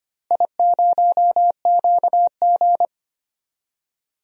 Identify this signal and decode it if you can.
Morse code, heard as a keyed tone, about 25 wpm, 700 Hz. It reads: I0QG
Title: DXCC-100 Call Signs - 4 Characters - Repeated 2x - 25wpm